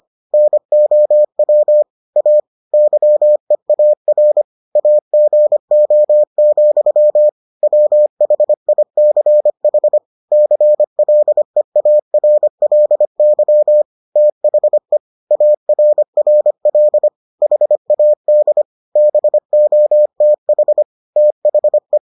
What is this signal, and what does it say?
Morse code, heard as a keyed tone, about 25 wpm, 600 Hz. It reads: NOW A YEAR AGO, WHICH CLEARLY THE ARRL HAD BOTH THE